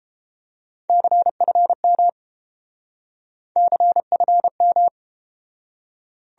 Morse code, heard as a keyed tone, about 31 wpm, 700 Hz. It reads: CFM CFM